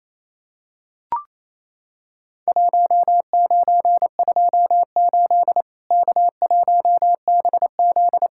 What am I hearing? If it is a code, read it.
Morse code, heard as a keyed tone, about 28 wpm, 700 Hz. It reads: 1928 K1BZ
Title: Worked All Europe DX Contest - QTC Exchange - 28wpm